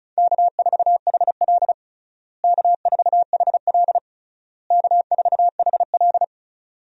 Morse code, heard as a keyed tone, about 36 wpm, 700 Hz. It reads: K4HL K4HL K4HL
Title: USA Call Signs - 4 Characters - Repeated 3x - 35wpm